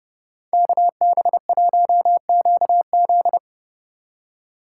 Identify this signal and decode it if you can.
Morse code, heard as a keyed tone, about 40 wpm, 700 Hz. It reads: KB1QZ